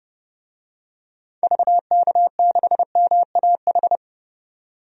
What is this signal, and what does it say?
Morse code, heard as a keyed tone, about 30 wpm, 700 Hz. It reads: VK6MAH
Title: DXCC-100 Call Signs - 6 Characters - 30wpm